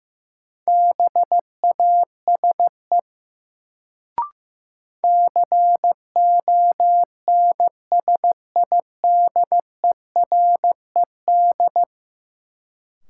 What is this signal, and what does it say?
Morse code, heard as a keyed tone, about 15 wpm, 700 Hz. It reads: BASE CONSIDERED
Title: Sets of 5 Words - New Word Drill - 15wpm